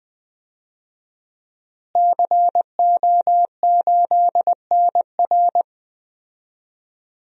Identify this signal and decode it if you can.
Morse code, heard as a keyed tone, about 21 wpm, 700 Hz. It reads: CO8NR
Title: DXCC-100 Call Signs - 5 Characters - Repeated 3x - 20wpm